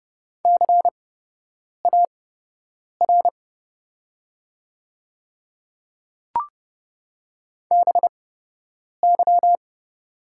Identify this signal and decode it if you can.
Morse code, heard as a keyed tone, about 30 wpm, spaced as for 8 wpm, 700 Hz. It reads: CAR BY
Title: 8x Character Spacing - Top 500 Words - 30wpm